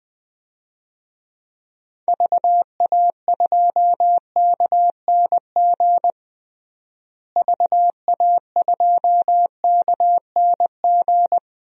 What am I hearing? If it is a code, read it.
Morse code, heard as a keyed tone, about 20 wpm, 700 Hz. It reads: VA2KNG VA2KNG
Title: DXCC-100 Call Signs - 6 Characters - Repeated 2x - 20wpm